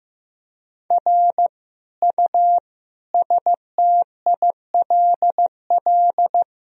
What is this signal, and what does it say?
Morse code, heard as a keyed tone, about 15 wpm, 700 Hz. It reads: R U STILL
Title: Sentences from Top 400 Words and Abbreviations 15wpm